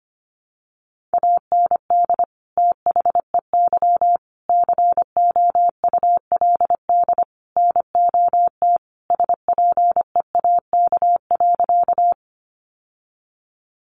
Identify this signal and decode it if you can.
Morse code, heard as a keyed tone, about 25 wpm, 700 Hz. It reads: AND THEY COULD NOT SPEAK.
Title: Sentences from Top 1100 Words 25wpm